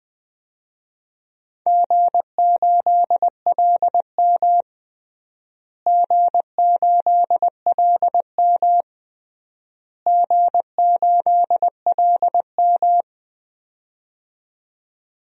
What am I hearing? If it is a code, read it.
Morse code, heard as a keyed tone, about 20 wpm, 700 Hz. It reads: G8LM G8LM G8LM
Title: DXCC-100 Call Signs - 4 Characters - Repeated 3x - 20wpm